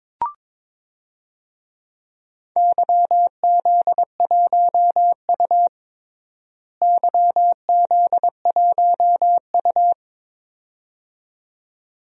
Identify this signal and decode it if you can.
Morse code, heard as a keyed tone, about 22 wpm, 700 Hz. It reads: YZ1U YZ1U
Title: DXCC-100 Call Signs - 4 Characters - Repeated 2x - 22wpm